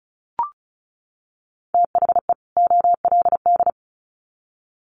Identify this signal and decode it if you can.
Morse code, heard as a keyed tone, about 34 wpm, 700 Hz. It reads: THE OLD
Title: Sets of 2 Words 35wpm